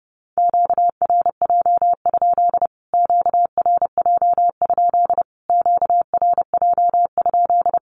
Morse code, heard as a keyed tone, about 30 wpm, 700 Hz. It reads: QRJ? QRJ? QRJ?
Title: Q-Codes Repeated 3x 30wpm